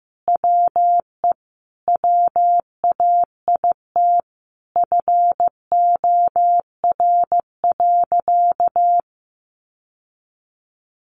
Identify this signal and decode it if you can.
Morse code, heard as a keyed tone, about 15 wpm, 700 Hz. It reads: WE WAIT FOR.
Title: Sentences from Top 800 Words 15wpm